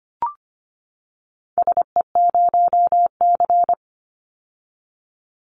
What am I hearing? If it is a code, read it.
Morse code, heard as a keyed tone, about 25 wpm, 700 Hz. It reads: SE0C
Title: Call Signs - 25wpm